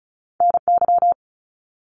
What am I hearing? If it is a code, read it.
Morse code, heard as a keyed tone, about 36 wpm, 700 Hz. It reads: NY
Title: US States 35wpm